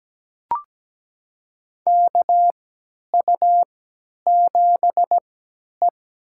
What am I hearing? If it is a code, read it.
Morse code, heard as a keyed tone, about 17 wpm, spaced as for 10 wpm, 700 Hz. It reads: KU7E
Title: USA Call Signs - 4 Characters - 3x Character Spacing - 17wpm